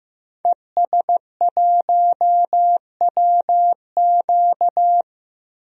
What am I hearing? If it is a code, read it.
Morse code, heard as a keyed tone, about 15 wpm, 700 Hz. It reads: ES1WQ
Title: DXCC-100 Call Signs - 5 Characters - 15wpm